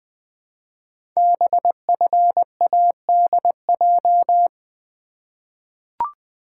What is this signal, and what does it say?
Morse code, heard as a keyed tone, about 20 wpm, 700 Hz. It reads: BFADJ
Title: Code Groups - 5 Letters - 20wpm